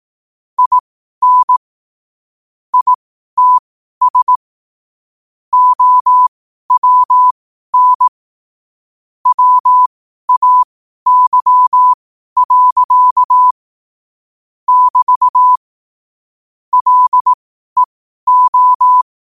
Morse code, heard as a keyed tone, about 18 wpm, spaced as for 12 wpm, 1000 Hz. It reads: IN ITS OWN WAY. <BT> LEO
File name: qotdfiles / 12wpm